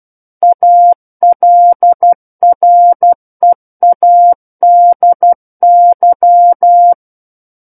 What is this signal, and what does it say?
Morse code, heard as a keyed tone, about 12 wpm, 700 Hz. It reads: ALREADY
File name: Book One / 03-Chapter 3 - 12 WPM